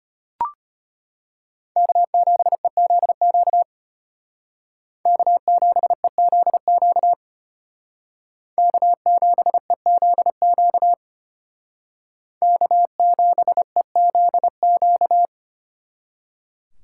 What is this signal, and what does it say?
Morse code, heard as a keyed tone, about 32 wpm, 700 Hz. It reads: K7EZQ K7EZQ K7EZQ K7EZQ